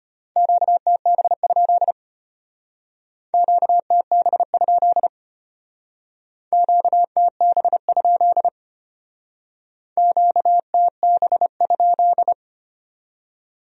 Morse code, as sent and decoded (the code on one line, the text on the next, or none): --.- - -... ..--.. / --.- - -... ..--.. / --.- - -... ..--.. / --.- - -... ..--..
QTB? QTB? QTB? QTB?